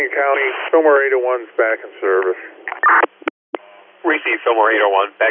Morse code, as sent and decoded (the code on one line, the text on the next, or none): none